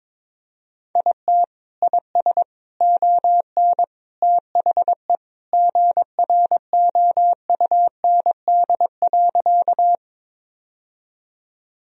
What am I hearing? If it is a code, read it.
Morse code, heard as a keyed tone, about 22 wpm, 700 Hz. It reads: IT IS ON THE GROUND.
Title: Sentences from Top 1000 Words - Review Part 2 - 22wpm